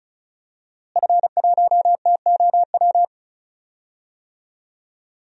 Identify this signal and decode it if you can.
Morse code, heard as a keyed tone, about 36 wpm, 700 Hz. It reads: F1TOW
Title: DXCC-100 Call Signs - 5 Characters - Repeated 2x - 35wpm